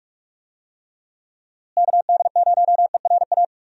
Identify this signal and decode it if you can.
Morse code, heard as a keyed tone, about 46 wpm, 700 Hz. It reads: KD0ERA
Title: USA Call Signs - 6 Characters - Repeated 2x - 45wpm